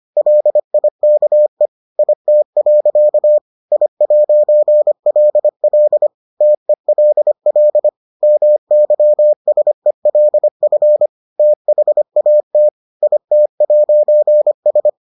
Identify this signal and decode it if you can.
Morse code, heard as a keyed tone, about 25 wpm, 600 Hz. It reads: LIKE IT. I'LL TELL MYSELF THAT IT'S